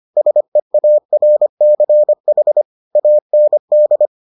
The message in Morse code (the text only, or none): SEARCH AND